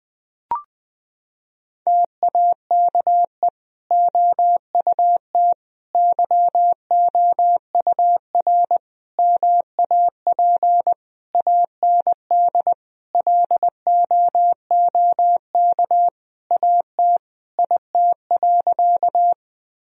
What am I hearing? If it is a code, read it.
Morse code, heard as a keyed tone, about 20 wpm, 700 Hz. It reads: TAKE OUT YOUR MAP AND LOOK AT IT.
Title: Sentences from Top 1000 Words - Review Part 2 - 20wpm